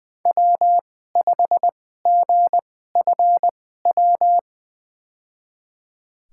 Morse code, heard as a keyed tone, about 21 wpm, spaced as for 15 wpm, 700 Hz. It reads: W5GFW
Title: USA Call Signs - 2x Character Spacing - 20wpm